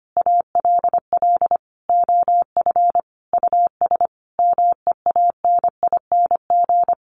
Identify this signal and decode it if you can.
Morse code, heard as a keyed tone, about 25 wpm, 700 Hz. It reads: ALL OF US MEANING
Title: A Sound Of Thunder 25wpm